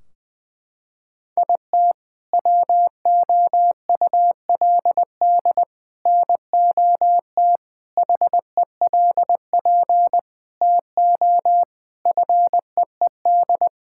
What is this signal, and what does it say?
Morse code, heard as a keyed tone, about 20 wpm, 700 Hz. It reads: IT WOULD NOT HELP TO FEED
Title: Sentences from Top 1600 Words 20wpm